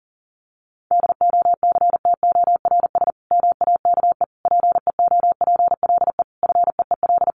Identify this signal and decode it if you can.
Morse code, heard as a keyed tone, about 40 wpm, 700 Hz. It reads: DOCTORS MAKE PEOPLE FEEL
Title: Easy Phrases 40wpm